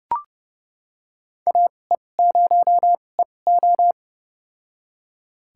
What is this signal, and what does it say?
Morse code, heard as a keyed tone, about 30 wpm, spaced as for 22 wpm, 700 Hz. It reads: AE0EO
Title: USA Call Signs - 5 Characters - 2x Character Spacing - 30wpm